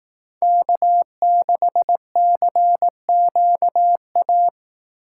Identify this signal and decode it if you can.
Morse code, heard as a keyed tone, about 18 wpm, 700 Hz. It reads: K6CQA
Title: Speed Racing - USA 1x3 Call Signs 15wpm